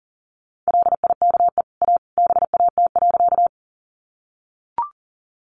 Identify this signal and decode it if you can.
Morse code, heard as a keyed tone, about 37 wpm, 700 Hz. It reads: LIKE A BAT.
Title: Sentences from Top 1500 Words 40wpm